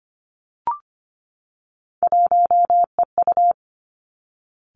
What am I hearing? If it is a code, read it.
Morse code, heard as a keyed tone, about 25 wpm, 700 Hz. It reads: 1EU
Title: Three Challenging Letter-Number Combinations 25wpm